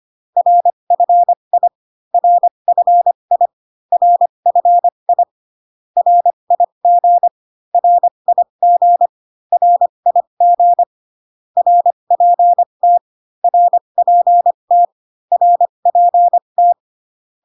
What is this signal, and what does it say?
Morse code, heard as a keyed tone, about 25 wpm, spaced as for 22 wpm, 700 Hz. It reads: RFI RFI RFI RIG RIG RIG RPT RPT RPT
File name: CW common abbreviations alphabetical repeated 3 times 25 18wpm